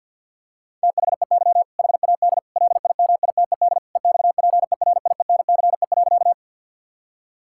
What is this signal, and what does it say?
Morse code, heard as a keyed tone, about 50 wpm, 700 Hz. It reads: THEY HAD LIMITED EXPERIENCE.
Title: Sentences from Top 1000 Words - Review Part 6 - 50wpm